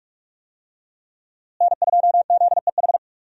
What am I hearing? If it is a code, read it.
Morse code, heard as a keyed tone, about 46 wpm, 700 Hz. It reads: N2ZEH
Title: USA Call Signs - 5 Characters - Repeated 2x - 45wpm